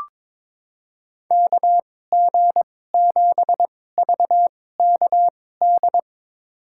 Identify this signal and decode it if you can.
Morse code, heard as a keyed tone, about 22 wpm, spaced as for 16 wpm, 700 Hz. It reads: KG7VKD
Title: USA Call Signs - 6 Characters - 2x Character Spacing - 22wpm